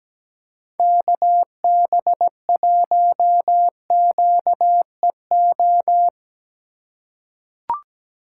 Morse code, heard as a keyed tone, about 17 wpm, 700 Hz. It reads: KB1QEO
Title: USA Call Signs - 6 Characters - 2x Character Spacing - 17wpm